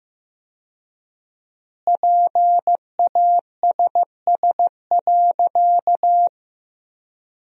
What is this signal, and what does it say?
Morse code, heard as a keyed tone, about 15 wpm, 700 Hz. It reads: PASS.
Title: 4x Word-Spacing - Sentences from Top 400 Words - 15wpm